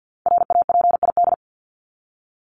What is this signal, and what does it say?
Morse code, heard as a keyed tone, about 50 wpm, 700 Hz. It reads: RAPID
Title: LY Suffix - Top 100 Words with Base Words 50wpm